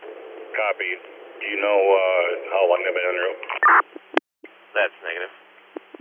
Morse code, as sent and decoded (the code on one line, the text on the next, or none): none